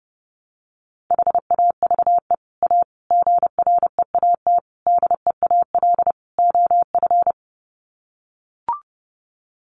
Morse code, heard as a keyed tone, about 30 wpm, 700 Hz. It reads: HAVE A GREAT DEAL OF
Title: Sets of 5 Words Encore2 30wpm